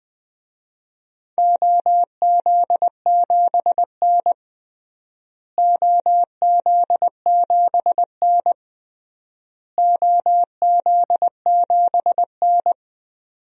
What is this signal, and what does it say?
Morse code, heard as a keyed tone, about 20 wpm, 700 Hz. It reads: OZ7N OZ7N OZ7N
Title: DXCC-100 Call Signs - 4 Characters - Repeated 3x - 20wpm